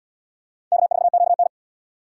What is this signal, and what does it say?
Morse code, heard as a keyed tone, about 78 wpm, 700 Hz. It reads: K4QN